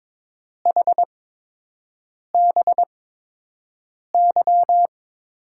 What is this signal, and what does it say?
Morse code, heard as a keyed tone, about 21 wpm, spaced as for 6 wpm, 700 Hz. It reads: HBY
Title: USA Call Signs - 5 Characters - 8x Character Spacing - 22wpm